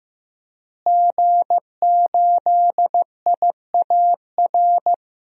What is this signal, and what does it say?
Morse code, heard as a keyed tone, about 16 wpm, 700 Hz. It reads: G8IAR
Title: DXCC-100 Call Signs - 5 Characters - Repeated 2x - 15wpm